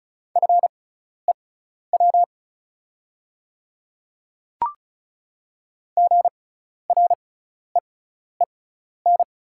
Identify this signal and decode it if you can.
Morse code, heard as a keyed tone, about 36 wpm, spaced as for 12 wpm, 700 Hz. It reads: FEW GREEN
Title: 6x Character Spacing - Top 500 Words - 35wpm